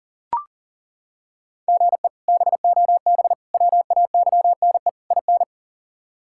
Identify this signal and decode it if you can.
Morse code, heard as a keyed tone, about 40 wpm, 700 Hz. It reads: GE BOB WAYNE IN
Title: K1USN SST QSO Practice - 40wpm